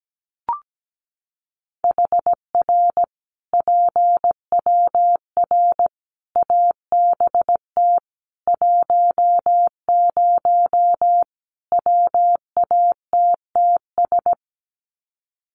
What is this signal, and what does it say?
Morse code, heard as a keyed tone, about 17 wpm, 700 Hz. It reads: HR PWR ABT 10 WATTS
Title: Common QSO Phrases 17wpm